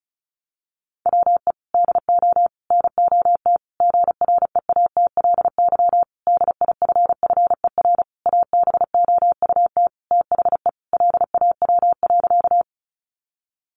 Morse code, heard as a keyed tone, about 34 wpm, 700 Hz. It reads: WE DO NOT GREATLY DIFFER ABOUT THE LAW.
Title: Sentences from Top 1400 Words 35wpm